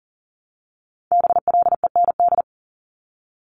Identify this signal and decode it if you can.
Morse code, heard as a keyed tone, about 42 wpm, 700 Hz. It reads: BLEND
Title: Multi-letter Phonemes Repeated 3x 40wpm